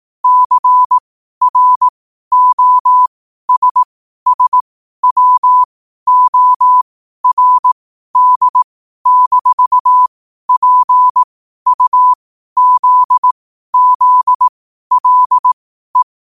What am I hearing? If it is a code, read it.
Morse code, heard as a keyed tone, about 18 wpm, spaced as for 12 wpm, 1000 Hz. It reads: CROSSWORD-PUZZLE
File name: qotdfiles / 12wpm